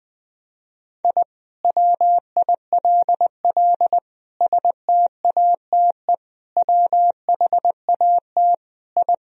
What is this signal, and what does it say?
Morse code, heard as a keyed tone, about 20 wpm, 700 Hz. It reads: I WILL STATE WHAT I
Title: Sentences from Top 1000 Words - Review Part 2 - 20wpm